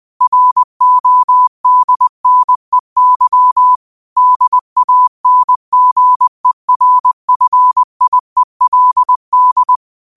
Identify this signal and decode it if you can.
Morse code, heard as a keyed tone, about 20 wpm, 1000 Hz. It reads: RODNEY DANGERFIELD
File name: qotdfiles / 20wpm